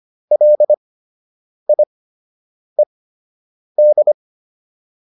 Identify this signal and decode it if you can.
Morse code, heard as a keyed tone, about 25 wpm, spaced as for 8 wpm, 600 Hz. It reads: LIED
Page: fundamental-practice-files